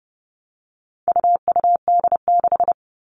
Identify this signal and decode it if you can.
Morse code, heard as a keyed tone, about 30 wpm, 700 Hz. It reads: UUD6
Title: All Cut Numbers - 4-Digit 30wpm